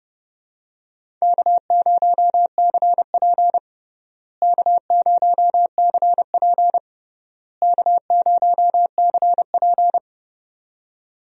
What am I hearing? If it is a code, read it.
Morse code, heard as a keyed tone, about 30 wpm, 700 Hz. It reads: K0CP K0CP K0CP
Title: USA Call Signs - 4 Characters - Repeated 3x - 30wpm